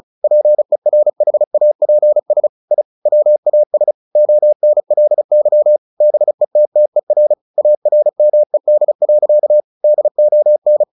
Morse code, heard as a keyed tone, about 34 wpm, 600 Hz. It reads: PERHAPS I WAS ONLY BETTER ARMED. DON
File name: the old man and the sea 2 35wpm